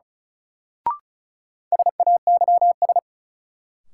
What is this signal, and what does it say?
Morse code, heard as a keyed tone, about 36 wpm, 700 Hz. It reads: SAYS
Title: Sets of 2 Words - New Word Drill - 35wpm